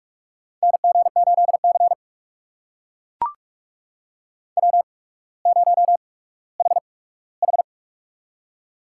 Morse code, heard as a keyed tone, about 44 wpm, spaced as for 19 wpm, 700 Hz. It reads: NG8C W0HH
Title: USA Call Signs - 4 Characters - 8x Character Spacing - 45wpm